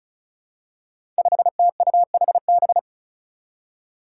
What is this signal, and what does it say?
Morse code, heard as a keyed tone, about 36 wpm, 700 Hz. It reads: 5TUHB
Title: Five Challenging Letter-Number Combinations 35wpm